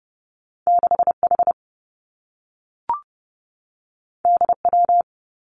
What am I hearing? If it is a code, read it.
Morse code, heard as a keyed tone, about 30 wpm, 700 Hz. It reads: 6 H D W
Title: Two Challenging Letter-Number Combinations 30wpm